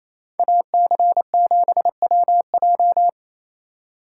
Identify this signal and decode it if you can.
Morse code, heard as a keyed tone, about 35 wpm, 700 Hz. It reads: AC7WJ
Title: Speed Racing - USA 2x2 Call Signs 28wpm